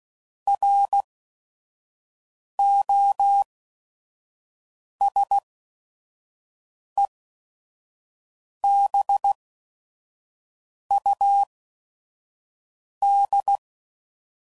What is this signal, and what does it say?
Morse code, heard as a keyed tone, about 16 wpm, spaced as for 5 wpm, 800 Hz. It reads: ROSEBUD
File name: CW Code Groups @5WPM / fromSM's / Cities